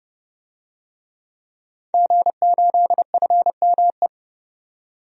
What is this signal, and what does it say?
Morse code, heard as a keyed tone, about 30 wpm, 700 Hz. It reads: G8FME
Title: DXCC-100 Call Signs - 5 Characters - 30wpm